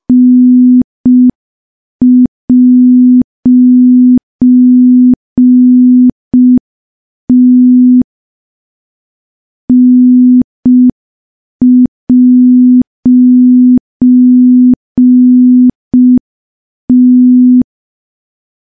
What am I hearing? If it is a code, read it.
Morse code, heard as a keyed tone, about 5 wpm, 250 Hz. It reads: N'T N'T
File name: morse / morse-training-05WPM